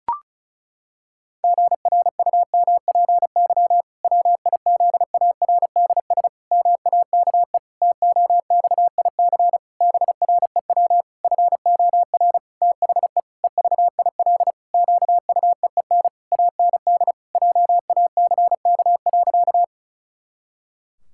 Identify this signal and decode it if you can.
Morse code, heard as a keyed tone, about 36 wpm, 700 Hz. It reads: GRUMPY WIZARDS MAKE TOXIC BREW FOR THE EVIL QUEEN AND JACK.
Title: Pangram Challenge v2 - 35wpm